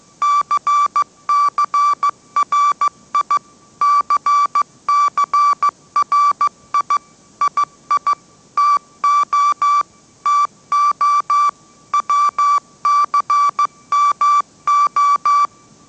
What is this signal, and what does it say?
Morse code, heard as a keyed tone, about 15 wpm, 1160 Hz. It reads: CCRI CCRI II TO TO WCMO
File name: mrx / mrx066 16wpm